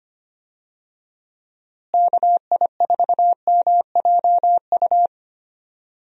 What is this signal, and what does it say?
Morse code, heard as a keyed tone, about 25 wpm, 700 Hz. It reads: KI4MJU
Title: USA Call Signs - 6 Characters - Repeated 3x - 25wpm